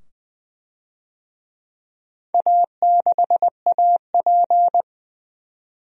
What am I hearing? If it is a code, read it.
Morse code, heard as a keyed tone, about 20 wpm, 700 Hz. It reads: A6AP
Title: DXCC-100 Call Signs - 4 Characters - 20wpm